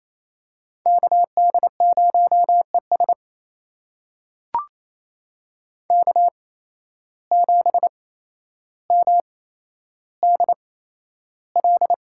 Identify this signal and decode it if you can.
Morse code, heard as a keyed tone, about 27 wpm, spaced as for 12 wpm, 700 Hz. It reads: KD0ES K7MDL
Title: USA Call Signs - 8x Character Spacing - 28wpm